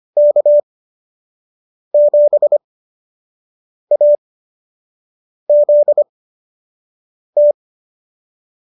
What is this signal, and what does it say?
Morse code, heard as a keyed tone, about 25 wpm, spaced as for 6 wpm, 590 Hz. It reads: K7AZT